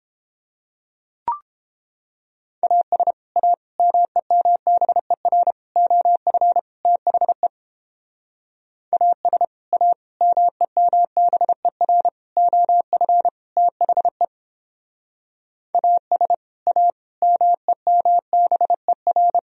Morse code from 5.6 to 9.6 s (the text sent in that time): OF THE AS